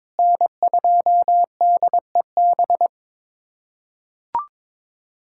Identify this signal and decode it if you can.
Morse code, heard as a keyed tone, about 22 wpm, 700 Hz. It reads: N2DEB
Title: USA Callsign DE Callsign 22wpm